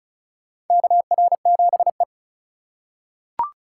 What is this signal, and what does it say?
Morse code, heard as a keyed tone, about 36 wpm, 700 Hz. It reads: KR7E
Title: USA Call Signs - 4 Characters - 2x Character Spacing - 35wpm